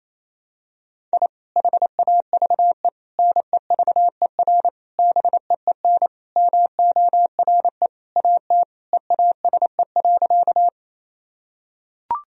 Morse code, heard as a keyed tone, about 28 wpm, 700 Hz. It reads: I HAVE NEVER BEEN MORE AT EASE.
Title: Sentences from Top 1000 Words - Review Part 1 - 28wpm